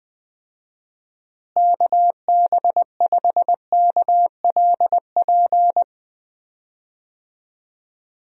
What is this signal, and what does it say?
Morse code, heard as a keyed tone, about 20 wpm, 700 Hz. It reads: KB5KLP
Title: USA Call Signs - 6 Characters - Repeated 2x - 20wpm